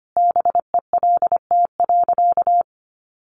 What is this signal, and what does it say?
Morse code, heard as a keyed tone, about 25 wpm, 700 Hz. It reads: BELT.
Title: Sentences from Top 1400 Words 25wpm